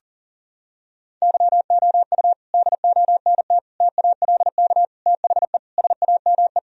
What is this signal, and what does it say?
Morse code, heard as a keyed tone, about 40 wpm, 700 Hz. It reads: YOU DONT TALK THE SAME
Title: Sentences from Top 1000 Words - Review Part 3 - 40wpm